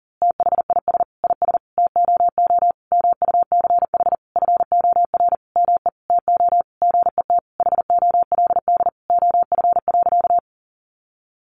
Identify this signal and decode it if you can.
Morse code, heard as a keyed tone, about 40 wpm, 700 Hz. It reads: THIS IS TOO MUCH FOR ME TO GET HOLD OF.
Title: A Sound Of Thunder 40wpm